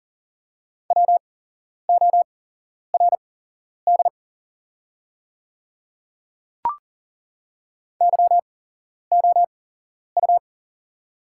Morse code, heard as a keyed tone, about 40 wpm, spaced as for 11 wpm, 700 Hz. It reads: WORD YOU
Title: 8x Character Spacing - Top 100 Words - 40wpm